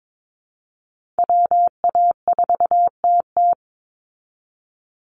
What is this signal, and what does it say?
Morse code, heard as a keyed tone, about 22 wpm, 700 Hz. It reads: WA4TT
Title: Call Signs Repeated 3x - 22wpm